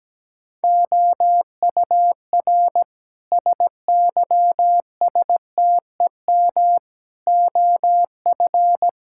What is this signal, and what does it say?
Morse code, heard as a keyed tone, about 17 wpm, 700 Hz. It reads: OUR SYSTEM OF
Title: Sentences from Top 1000 Words - Review Part 6 - 17wpm